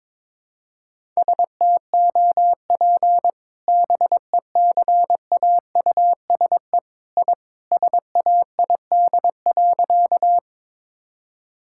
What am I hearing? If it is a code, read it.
Morse code, heard as a keyed tone, about 21 wpm, 700 Hz. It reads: STOP BECAUSE I SAID.
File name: practice / 22wpm / Sentences with 2-Word Phrases - 22wpm